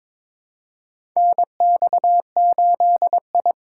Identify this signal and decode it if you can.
Morse code, heard as a keyed tone, about 22 wpm, 700 Hz. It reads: NX8I
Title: USA Call Signs - 4 Characters - 3x Character Spacing - 22wpm